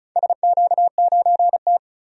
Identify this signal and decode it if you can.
Morse code, heard as a keyed tone, about 36 wpm, 700 Hz. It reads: SQ9T
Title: DXCC-100 Call Signs - 4 Characters - 35wpm